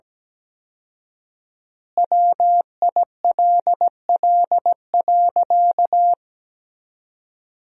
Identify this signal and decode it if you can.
Morse code, heard as a keyed tone, about 17 wpm, 700 Hz. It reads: WILL.
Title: 4x Word-Spacing - Sentences from Top 500 Words - 17wpm